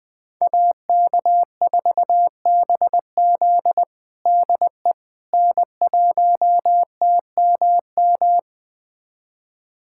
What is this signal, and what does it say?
Morse code, heard as a keyed tone, about 20 wpm, 700 Hz. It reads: AK4BZ DE N1TMM
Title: USA Callsign DE Callsign - 5 Characters - 20wpm